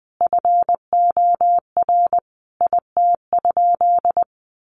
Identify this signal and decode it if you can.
Morse code, heard as a keyed tone, about 20 wpm, 700 Hz. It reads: FOR IT?
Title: Sentences from Top 1200 Words 20wpm